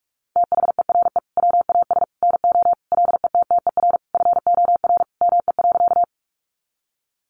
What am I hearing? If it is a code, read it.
Morse code, heard as a keyed tone, about 44 wpm, 700 Hz. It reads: THERE WAS NO LETTER FOR ME.
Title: Sentences from Top 300 Words 45wpm